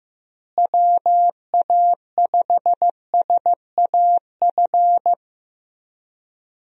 Morse code, heard as a keyed tone, about 15 wpm, 700 Hz. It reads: WA5SAF
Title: USA Call Signs - 6 Characters - Repeated 2x - 15wpm